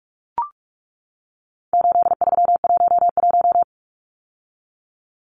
Morse code, heard as a keyed tone, about 44 wpm, 700 Hz. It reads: 8311
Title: Numbers - 4-Digit 45wpm